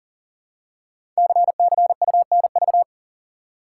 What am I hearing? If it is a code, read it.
Morse code, heard as a keyed tone, about 39 wpm, 690 Hz. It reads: CCUNV
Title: Code Groups - 5 Letters - 40wpm